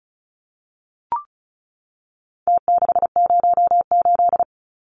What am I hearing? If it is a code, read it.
Morse code, heard as a keyed tone, about 34 wpm, 700 Hz. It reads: T608
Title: Numbers - Contest Style - 4-Digit 35wpm